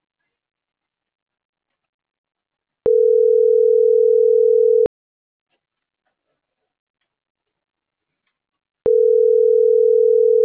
Ringback tone (North America), 440 + 480 Hz.